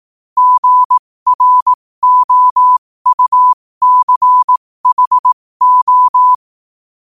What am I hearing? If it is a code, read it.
Morse code, heard as a keyed tone, about 18 wpm, spaced as for 15 wpm, 1000 Hz. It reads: GROUCHO
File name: qotdfiles / 15wpm